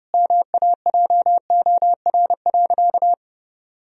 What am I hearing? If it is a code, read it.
Morse code, heard as a keyed tone, about 30 wpm, 700 Hz. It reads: MAJOR.
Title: Sentences from Top 1000 Words - Review Part 4 - 30wpm